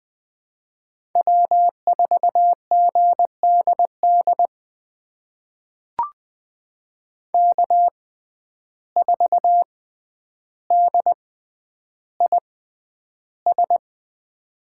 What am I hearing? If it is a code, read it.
Morse code, heard as a keyed tone, about 20 wpm, spaced as for 10 wpm, 700 Hz. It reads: W4GDD K4DIS
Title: USA Call Signs - 5 Characters - 6x Character Spacing - 20wpm